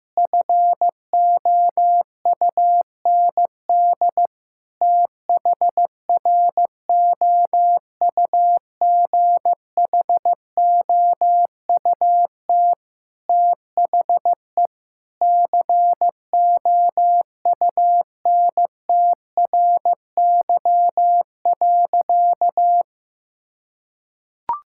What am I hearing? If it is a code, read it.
Morse code, heard as a keyed tone, about 16 wpm, 700 Hz. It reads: FOUND THROUGHOUT THE COUNTRY.
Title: Sentences from Top 1000 Words - Review Part 5 - 15wpm